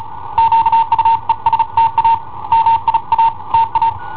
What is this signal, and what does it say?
Morse code, heard as a keyed tone, about 35 wpm, 910 Hz. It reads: QUESTA MIA TA
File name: i7ohp / cassetta i7ohp B